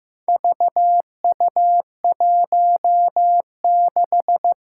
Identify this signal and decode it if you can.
Morse code, heard as a keyed tone, about 15 wpm, 700 Hz. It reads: VU16
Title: Four Challenging Letter-Number Combinations 15wpm